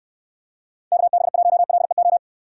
Morse code, heard as a keyed tone, about 64 wpm, 700 Hz. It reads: KB1LEO